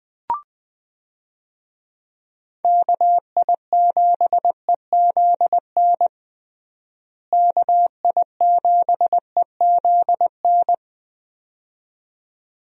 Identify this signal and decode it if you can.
Morse code, heard as a keyed tone, about 20 wpm, 700 Hz. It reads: KI7EZN KI7EZN
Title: USA Call Signs - 6 Characters - Repeated 2x - 20wpm